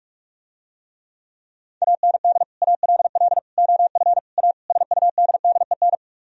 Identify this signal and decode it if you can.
Morse code, heard as a keyed tone, about 46 wpm, 700 Hz. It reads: AND ALL OF A SUDDEN